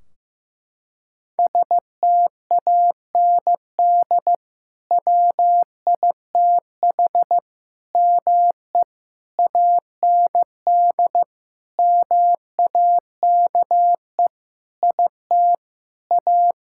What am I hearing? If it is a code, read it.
Morse code, heard as a keyed tone, about 15 wpm, 700 Hz. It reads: STAND WITH ME AND MAKE IT A
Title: Sentences from Top 1600 Words 15wpm